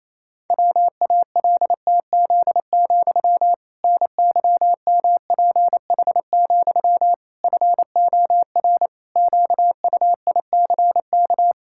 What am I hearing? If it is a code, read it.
Morse code, heard as a keyed tone, about 27 wpm, 700 Hz. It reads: WALTZ, NYMPH, FOR QUICK
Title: Pangram Challenge v2 - 28wpm